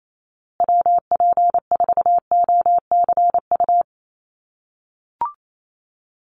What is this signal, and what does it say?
Morse code, heard as a keyed tone, about 28 wpm, 700 Hz. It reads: WP4OCU
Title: USA Call Signs Repeated 3x 28wpm